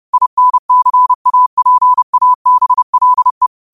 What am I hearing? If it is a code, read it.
Morse code, heard as a keyed tone, about 30 wpm, 1000 Hz. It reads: INCAPABLE
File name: qotdfiles / 30wpm